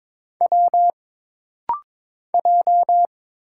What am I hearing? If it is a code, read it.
Morse code, heard as a keyed tone, about 22 wpm, 700 Hz. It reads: W J
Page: This rapid-fire timing is geared to push your ICR (Instant Character Recognition) skills.